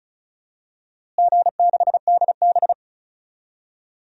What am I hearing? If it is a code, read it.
Morse code, heard as a keyed tone, about 36 wpm, 700 Hz. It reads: G6DB
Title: DXCC-100 Call Signs - 4 Characters - Repeated 2x - 35wpm